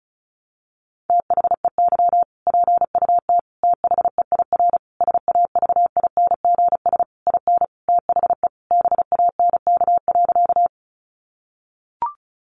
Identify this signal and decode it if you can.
Morse code, heard as a keyed tone, about 34 wpm, 700 Hz. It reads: THEY PUT THEIR SAVINGS IN THE BANK.
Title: Sentences from Top 1200 Words 35wpm